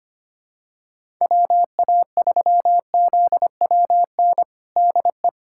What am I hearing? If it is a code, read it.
Morse code, heard as a keyed tone, about 25 wpm, 700 Hz. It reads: WA3ZWN DE
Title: USA Callsign DE Callsign 25wpm